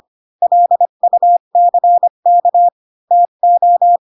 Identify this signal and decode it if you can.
Morse code, heard as a keyed tone, about 25 wpm, 700 Hz. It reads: LUCK TO
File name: Pinocchio / Pinocchio ch5 20 25 ik1hge